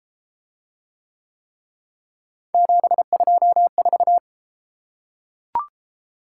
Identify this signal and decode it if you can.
Morse code, heard as a keyed tone, about 31 wpm, 700 Hz. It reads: 724